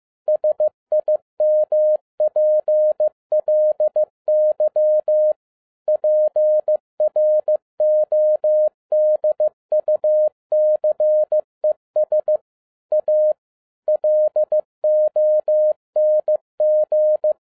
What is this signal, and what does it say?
Morse code, heard as a keyed tone, about 15 wpm, 610 Hz. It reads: SIMPLY PRODUCES A LONG